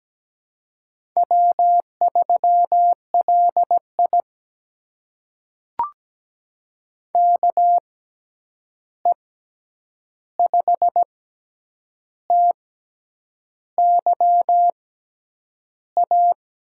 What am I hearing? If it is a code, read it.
Morse code, heard as a keyed tone, about 17 wpm, spaced as for 8 wpm, 700 Hz. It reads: W3LI KE5TYA
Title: USA Call Signs - 6x Character Spacing - 17wpm